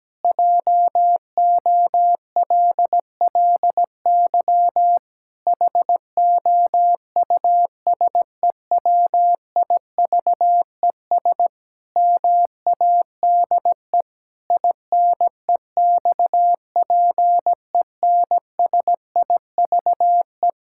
Morse code, heard as a keyed tone, about 17 wpm, 700 Hz. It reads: JOLLY HOUSEWIVES MADE INEXPENSIVE
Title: Pangram Challenge v2 - 17wpm